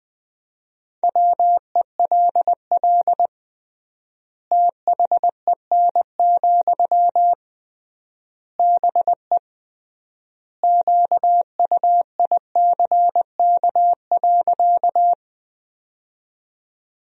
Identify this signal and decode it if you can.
Morse code, heard as a keyed tone, about 20 wpm, 700 Hz. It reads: WELL THEN, BE QUICK.
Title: 3x Word-Spacing - Sentences from Top 500 Words - 20wpm